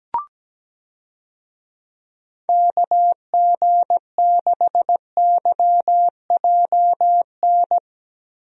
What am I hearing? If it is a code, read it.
Morse code, heard as a keyed tone, about 17 wpm, 700 Hz. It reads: KG6YJN
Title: USA Call Signs - 6 Characters - Repeated 2x - 17wpm